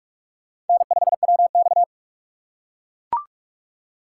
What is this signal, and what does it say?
Morse code, heard as a keyed tone, about 42 wpm, 700 Hz. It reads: N5WX